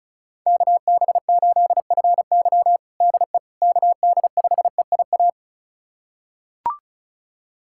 Morse code, heard as a keyed tone, about 34 wpm, 700 Hz. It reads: KB8FY DE KD5EIA
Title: USA Callsign DE Callsign 35wpm